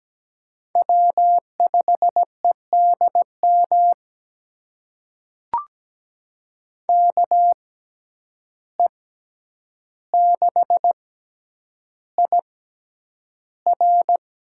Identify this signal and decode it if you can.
Morse code, heard as a keyed tone, about 17 wpm, spaced as for 9 wpm, 700 Hz. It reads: W5EDM KE6IR
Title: USA Call Signs - 5 Characters - 6x Character Spacing - 17wpm